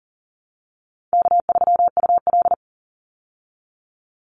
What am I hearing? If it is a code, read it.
Morse code, heard as a keyed tone, about 40 wpm, 700 Hz. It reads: K3UL
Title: Call Signs Repeated 2x - 40wpm